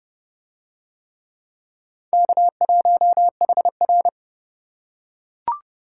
Morse code, heard as a keyed tone, about 27 wpm, 700 Hz. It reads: K1HR